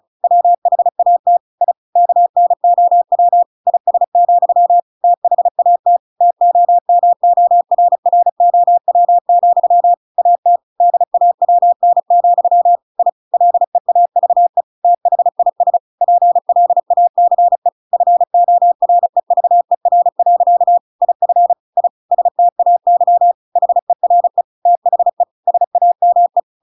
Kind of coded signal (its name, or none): Morse code